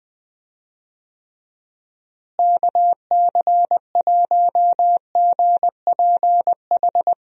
Morse code, heard as a keyed tone, about 21 wpm, 700 Hz. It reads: KC1GPH
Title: USA Call Signs - 6 Characters - Repeated 3x - 20wpm